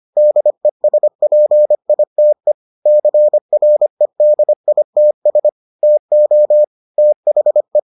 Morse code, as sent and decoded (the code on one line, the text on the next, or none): -.. ... .--. .. - . / -.-. .-. -.. .. - ... / - --- / - .... .
DESPITE CREDITS TO THE